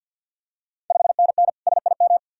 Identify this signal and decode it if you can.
Morse code, heard as a keyed tone, about 52 wpm, 690 Hz. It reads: 5NN SIM
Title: Ontario QSO Party Exchange 50wpm